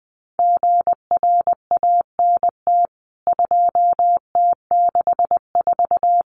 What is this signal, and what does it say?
Morse code, heard as a keyed tone, about 20 wpm, 700 Hz. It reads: GRANT 2T64
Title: CWT Exchange 20wpm